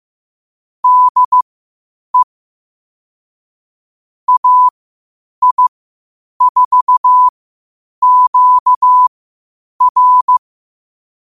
Morse code, heard as a keyed tone, about 15 wpm, spaced as for 8 wpm, 1000 Hz. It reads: DE AI4QR
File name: qotdfiles / 8wpm